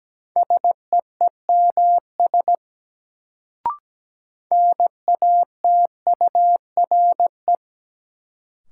Morse code, heard as a keyed tone, about 17 wpm, 700 Hz. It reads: SEEMS NATURE
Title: Sets of 3 Words - New Word Drill - 17wpm